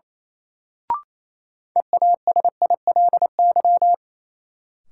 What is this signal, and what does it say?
Morse code, heard as a keyed tone, about 28 wpm, 700 Hz. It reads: EASILY
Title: Sentences from Top 900 Words - New Word Drill - 28wpm